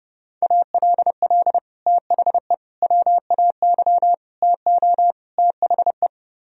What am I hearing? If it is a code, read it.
Morse code, heard as a keyed tone, about 30 wpm, 700 Hz. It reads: ALL THE WAY TO THE